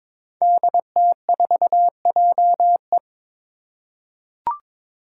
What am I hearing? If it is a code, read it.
Morse code, heard as a keyed tone, about 22 wpm, 700 Hz. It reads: DT4JE
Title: Five Challenging Letter-Number Combinations 22wpm